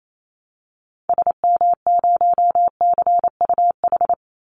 Morse code, heard as a keyed tone, about 28 wpm, 700 Hz. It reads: SM0CUH
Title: Call Signs Repeated 3x - 28wpm